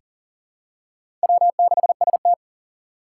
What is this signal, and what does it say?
Morse code, heard as a keyed tone, about 40 wpm, 700 Hz. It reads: W6ST
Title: USA Call Signs - 4 Characters - 2x Character Spacing - 40wpm